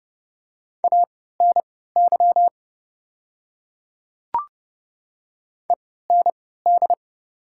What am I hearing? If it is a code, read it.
Morse code, heard as a keyed tone, about 30 wpm, spaced as for 17 wpm, 690 Hz. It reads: ANY END
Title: 3x Character Spacing - Top 200 Words - 30wpm